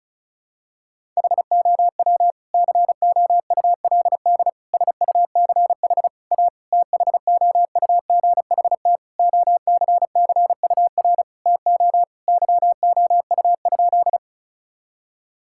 Morse code, heard as a keyed tone, about 35 wpm, 700 Hz. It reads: HOW COULD SUCH A THOUGHT OCCUR TO YOU?
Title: Sentences from Top 1000 Words - Review Part 6 - 35wpm